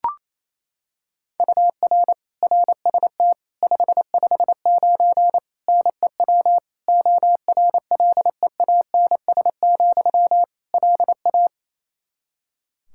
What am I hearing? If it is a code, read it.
Morse code, heard as a keyed tone, about 28 wpm, 700 Hz. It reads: UR RST 559 NEW ORLEANS, LA
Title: Practice QSO - Exchange 1 - Beginning - 28wpm